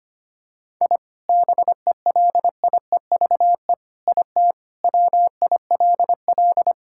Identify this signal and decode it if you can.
Morse code, heard as a keyed tone, about 25 wpm, 700 Hz. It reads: I BELIEVE IT WILL
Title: Sentences from Top 1000 Words - Review Part 5 - 25wpm